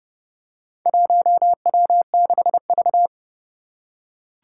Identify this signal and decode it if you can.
Morse code, heard as a keyed tone, about 28 wpm, 690 Hz. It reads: 1W6V
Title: Four Challenging Letter-Number Combinations 30wpm